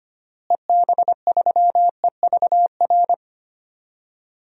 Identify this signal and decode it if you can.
Morse code, heard as a keyed tone, about 25 wpm, 700 Hz. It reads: EB3EVR
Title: DXCC-100 Call Signs - 6 Characters - Repeated 3x - 25wpm